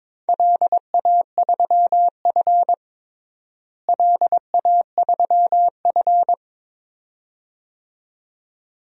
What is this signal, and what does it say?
Morse code, heard as a keyed tone, about 22 wpm, 700 Hz. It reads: LA3F LA3F
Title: DXCC-100 Call Signs - 4 Characters - Repeated 2x - 22wpm